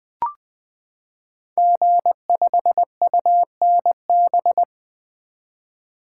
Morse code, heard as a keyed tone, about 20 wpm, 700 Hz. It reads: G5UNB
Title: Code Groups - 5 Letter:Numbers - 20wpm